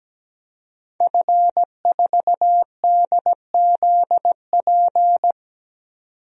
Morse code, heard as a keyed tone, about 17 wpm, 700 Hz. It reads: F4DZP
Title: DXCC-100 Call Signs - 5 Characters - 17wpm